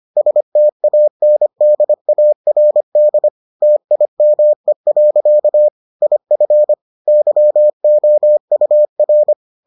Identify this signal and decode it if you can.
Morse code, heard as a keyed tone, about 25 wpm, 600 Hz. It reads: STANDARD TIME. IF YOUR